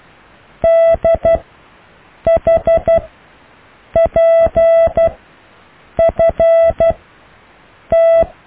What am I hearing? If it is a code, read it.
Morse code, heard as a keyed tone, about 12 wpm, spaced as for 7 wpm, 660 Hz. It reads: DHPFT
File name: cw / EKTPV+BFHNW